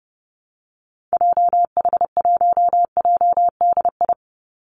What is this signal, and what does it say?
Morse code, heard as a keyed tone, about 30 wpm, 700 Hz. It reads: JH1JDI
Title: Call Signs Repeated 2x - 30wpm